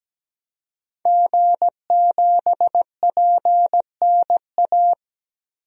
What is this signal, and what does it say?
Morse code, heard as a keyed tone, about 17 wpm, 700 Hz. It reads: G7PNA
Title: DXCC-100 Call Signs - 5 Characters - Repeated 3x - 17wpm